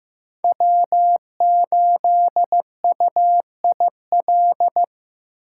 Morse code, heard as a keyed tone, about 15 wpm, 700 Hz. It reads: W8UIL
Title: K1USN SST QSO Practice - 15wpm